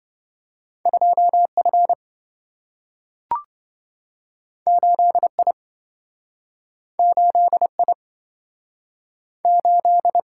Morse code, heard as a keyed tone, about 27 wpm, 700 Hz. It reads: 2F 8I 8I 8